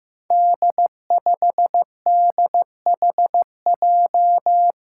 Morse code, heard as a keyed tone, about 15 wpm, 700 Hz. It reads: D5DHJ
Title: Five Challenging Letter-Number Combinations 15wpm